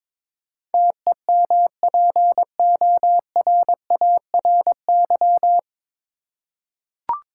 Morse code, heard as a keyed tone, about 22 wpm, 700 Hz. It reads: TEMPORARY
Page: Morse Code Ninja practice 22wpm